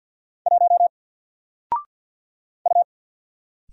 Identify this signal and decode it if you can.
Morse code, heard as a keyed tone, about 50 wpm, 700 Hz. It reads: 1 U
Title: Single Letter-Number 50wpm